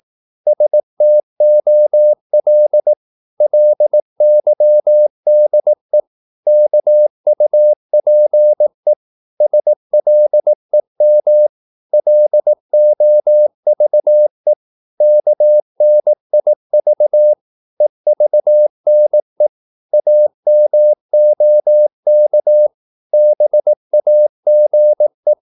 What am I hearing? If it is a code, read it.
Morse code, heard as a keyed tone, about 18 wpm, 600 Hz. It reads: STOL LYDE KUPE SLEM LOVE KNIV EVNE AMOK BAGE
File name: Korte ord DK 18wpm